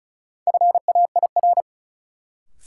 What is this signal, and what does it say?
Morse code, heard as a keyed tone, about 34 wpm, 700 Hz. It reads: FAIR
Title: Sentences from Top 1300 Words - New Word Drill Alt - 35wpm